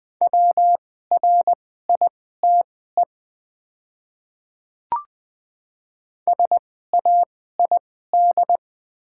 Morse code, heard as a keyed tone, about 20 wpm, spaced as for 14 wpm, 700 Hz. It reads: WRITE SAID
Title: 2x Character Spacing - Top 100 Words - 20wpm